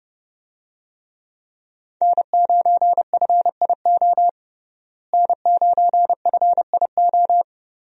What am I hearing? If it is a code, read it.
Morse code, heard as a keyed tone, about 31 wpm, 700 Hz. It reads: N9FIO N9FIO
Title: USA Call Signs - 5 Characters - Repeated 3x - 30wpm